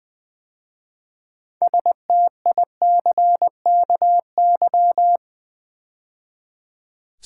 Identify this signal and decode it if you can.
Morse code, heard as a keyed tone, about 20 wpm, 700 Hz. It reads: STICKY
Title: 6 Letter Words - First 3 Letters Given - 20wpm